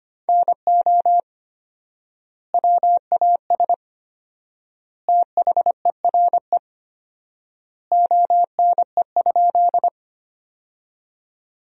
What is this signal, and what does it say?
Morse code, heard as a keyed tone, about 25 wpm, 700 Hz. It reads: NO WAS THERE ONE?
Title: 4x Word-Spacing - Sentences from Top 100 Words - 25wpm